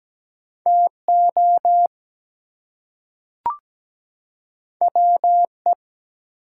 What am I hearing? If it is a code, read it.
Morse code, heard as a keyed tone, about 17 wpm, 700 Hz. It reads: TO WE
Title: Two Letter Words - 17wpm